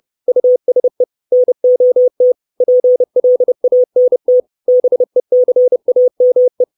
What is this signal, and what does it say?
Morse code, heard as a keyed tone, about 30 wpm, 500 Hz. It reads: USE NOT PLANT BECAME
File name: 632 Words 30WPM